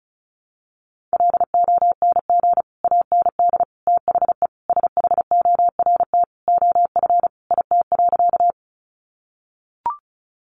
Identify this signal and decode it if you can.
Morse code, heard as a keyed tone, about 34 wpm, 700 Hz. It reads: LONG AND THE SHORT OF IT.
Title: Sentences from Top 400 Words 35wpm